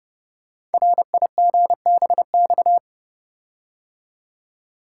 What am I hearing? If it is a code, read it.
Morse code, heard as a keyed tone, about 30 wpm, 700 Hz. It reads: RIGBX
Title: Code Groups - 5 Letters - 30wpm